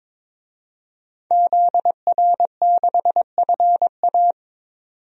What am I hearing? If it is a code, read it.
Morse code, heard as a keyed tone, about 22 wpm, 700 Hz. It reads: ZR6FA
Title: DXCC-100 Call Signs - 5 Characters - 22wpm